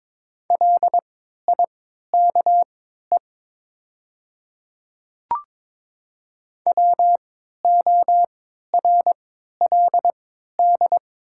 Morse code, heard as a keyed tone, about 22 wpm, spaced as for 13 wpm, 700 Hz. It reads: LIKE WORLD
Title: 3x Character Spacing - Top 300 Words - 22wpm